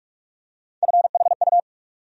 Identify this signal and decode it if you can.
Morse code, heard as a keyed tone, about 47 wpm, 700 Hz. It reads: FHU
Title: USA Call Sign Postfixes - 3 Letters 45wpm